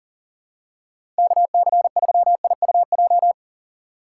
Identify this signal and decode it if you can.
Morse code, heard as a keyed tone, about 40 wpm, 700 Hz. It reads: KC3IUJ
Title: USA Call Signs - 4x Character Spacing - 40wpm